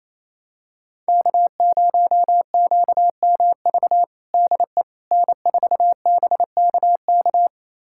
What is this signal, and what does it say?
Morse code, heard as a keyed tone, about 28 wpm, 700 Hz. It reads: K0QMV DE N4BKK
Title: USA Callsign DE Callsign - 5 Characters - 28wpm